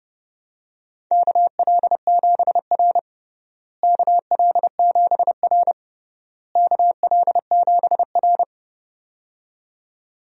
Morse code, heard as a keyed tone, about 31 wpm, 700 Hz. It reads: KL7R KL7R KL7R
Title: DXCC-100 Call Signs - 4 Characters - Repeated 3x - 30wpm